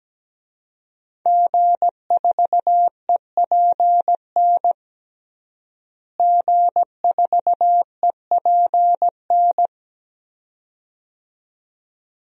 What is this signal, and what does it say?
Morse code, heard as a keyed tone, about 17 wpm, 700 Hz. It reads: G4EPN G4EPN
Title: DXCC-100 Call Signs - 5 Characters - Repeated 2x - 17wpm